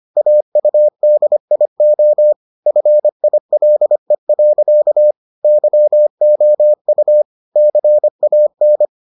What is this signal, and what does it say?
Morse code, heard as a keyed tone, about 25 wpm, 600 Hz. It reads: AUDIO FILE. YOU CAN